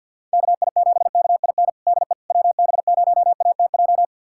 Morse code, heard as a keyed tone, about 46 wpm, 690 Hz. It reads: KI7KIN DE WB0ATJ
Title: USA Callsign DE Callsign 50wpm